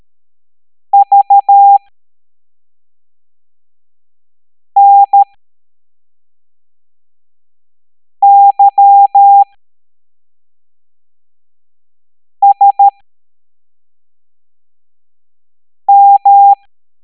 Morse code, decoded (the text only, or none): VNYSM